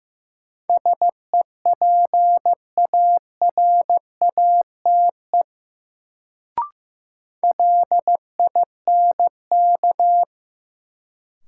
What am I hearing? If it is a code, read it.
Morse code, heard as a keyed tone, about 15 wpm, 700 Hz. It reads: SEPARATE LINK
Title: Sentences from Top 1000 Words - Review Part 7 - New Word Drill - 15wpm